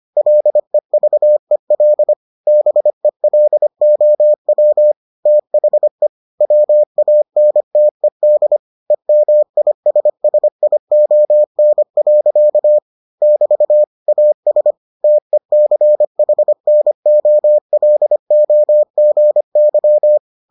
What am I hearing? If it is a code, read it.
Morse code, heard as a keyed tone, about 25 wpm, 600 Hz. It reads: LEVEL BELOW THE WANTED EMISSION. <BT> AS TECHNOLOGY